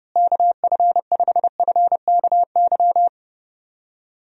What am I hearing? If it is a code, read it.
Morse code, heard as a keyed tone, about 30 wpm, 700 Hz. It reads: KF5FKY
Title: USA Call Signs - 6 Characters - Repeated 2x - 30wpm